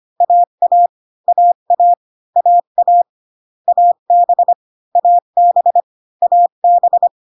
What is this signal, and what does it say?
Morse code, heard as a keyed tone, about 25 wpm, 700 Hz. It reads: AA AA AA AB AB AB